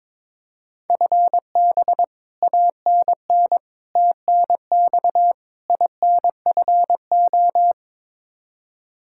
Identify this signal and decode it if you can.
Morse code, heard as a keyed tone, about 22 wpm, 700 Hz. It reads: FB ANN TNX INFO
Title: Practice QSO - Exchange 3 - More Info - 22wpm